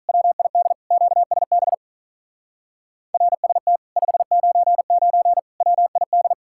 Morse code, heard as a keyed tone, about 46 wpm, 700 Hz. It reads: WID QSB RST 599 WID